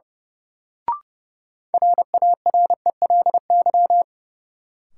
Morse code, heard as a keyed tone, about 32 wpm, 700 Hz. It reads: RARELY
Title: Sentences from Top 1400 Words - New Word Drill - 30wpm